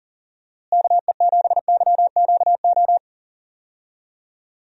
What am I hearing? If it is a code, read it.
Morse code, heard as a keyed tone, about 40 wpm, 700 Hz. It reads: KE7YQO
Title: USA Call Signs - 6 Characters - Repeated 2x - 40wpm